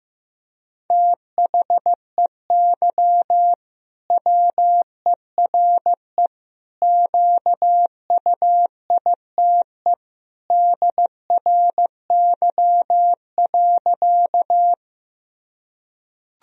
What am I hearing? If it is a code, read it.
Morse code, heard as a keyed tone, about 15 wpm, 700 Hz. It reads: THEY WERE QUITE DRY.
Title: Sentences from Top 1000 Words - Review Part 3 - 15wpm